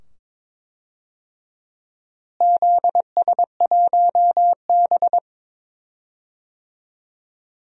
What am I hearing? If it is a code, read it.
Morse code, heard as a keyed tone, about 22 wpm, 700 Hz. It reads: ZS1B
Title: DXCC-100 Call Signs - 4 Characters - Repeated 2x - 22wpm